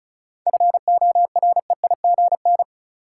Morse code, heard as a keyed tone, about 36 wpm, 700 Hz. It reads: FOREIGN
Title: Sentences from Top 1400 Words - New Word Drill - 35wpm